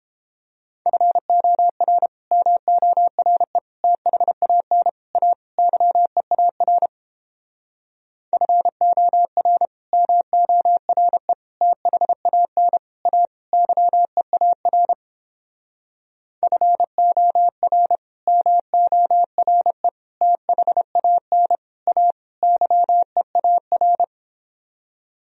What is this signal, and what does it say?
Morse code, heard as a keyed tone, about 29 wpm, 700 Hz. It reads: FOR MORE THAN A YEAR FOR MORE THAN A YEAR FOR MORE THAN A YEAR